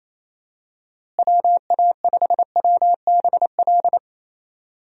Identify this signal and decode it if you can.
Morse code, heard as a keyed tone, about 27 wpm, 700 Hz. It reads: WA5WBL
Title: USA Call Signs - 6 Characters - Repeated 3x - 28wpm